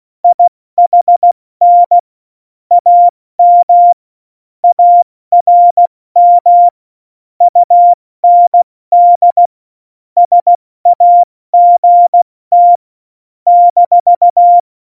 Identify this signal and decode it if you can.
Morse code, heard as a keyed tone, about 16 wpm, 700 Hz. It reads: IHN AM ARM UND SAGT -
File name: Morsekurs / Witze / Witz 22. Dezember 2020, 70 F80